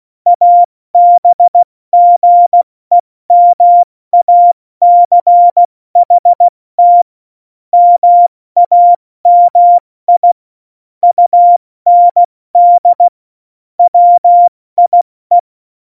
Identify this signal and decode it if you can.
Morse code, heard as a keyed tone, about 16 wpm, 700 Hz. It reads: ABGEMACHT MAMI UND WIE
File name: Morsekurs / Witze / Witz 08. August 17, Tempo 70 F80